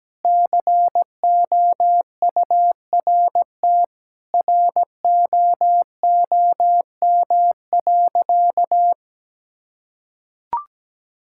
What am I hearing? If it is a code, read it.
Morse code, heard as a keyed tone, about 17 wpm, 700 Hz. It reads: COURT ROOM.
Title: Sentences from Top 1600 Words 17wpm